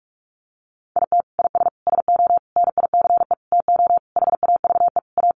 Morse code, heard as a keyed tone, about 44 wpm, 700 Hz. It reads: IT IS SO NICE TO HAVE A
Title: Sentences from Top 1100 Words 45wpm